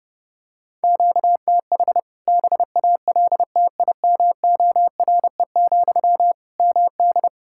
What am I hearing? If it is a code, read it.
Morse code, heard as a keyed tone, about 32 wpm, 700 Hz. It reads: QTH BALTIMORE, MD
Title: K1USN SST QSO Practice 30wpm